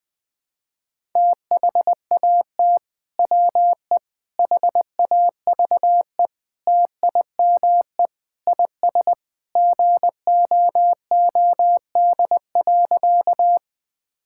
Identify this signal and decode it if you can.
Morse code, heard as a keyed tone, about 21 wpm, 700 Hz. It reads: THAT WE HAVE TIME IS GOOD.
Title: Sentences with 3-Word Phrases - 20wpm